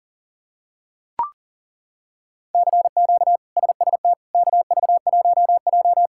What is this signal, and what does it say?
Morse code, heard as a keyed tone, about 40 wpm, 700 Hz. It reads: CQ SST KV1J
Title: K1USN SST QSO Practice - 40wpm